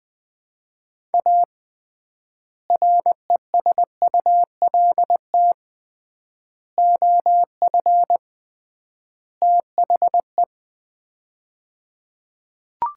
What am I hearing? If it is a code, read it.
Morse code, heard as a keyed tone, about 20 wpm, 700 Hz. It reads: A RESULT OF THE
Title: 3x Word-Spacing - Sets of 4 Words - 20wpm